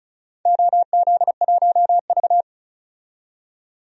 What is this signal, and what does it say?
Morse code, heard as a keyed tone, about 36 wpm, 700 Hz. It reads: OZ1V
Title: DXCC-100 Call Signs - 4 Characters - 35wpm